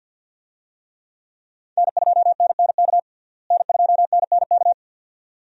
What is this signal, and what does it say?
Morse code, heard as a keyed tone, about 47 wpm, 700 Hz. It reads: N2NNK N2NNK
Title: USA Call Signs - 5 Characters - Repeated 2x - 50wpm